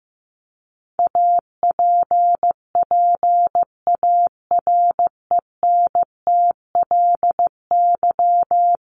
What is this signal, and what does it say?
Morse code, heard as a keyed tone, about 15 wpm, 700 Hz. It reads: APPARENTLY
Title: LY Suffix - Top 100 Words with Base Words 15wpm